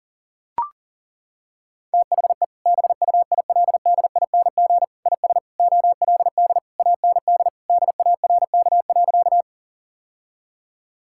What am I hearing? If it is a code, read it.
Morse code, heard as a keyed tone, about 40 wpm, 700 Hz. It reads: THE BUILDING IS OLD AND DARK.
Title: Sentences from Top 1000 Words - Review Part 5 - 40wpm